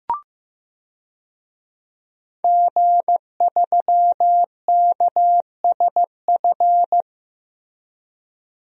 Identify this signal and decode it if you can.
Morse code, heard as a keyed tone, about 16 wpm, 700 Hz. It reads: G3KSF
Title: DXCC-100 Call Signs - 5 Characters - Repeated 3x - 15wpm